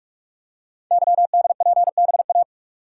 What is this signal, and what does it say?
Morse code, heard as a keyed tone, about 44 wpm, 690 Hz. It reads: YDPBA
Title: Code Groups - 5 Letters - 45wpm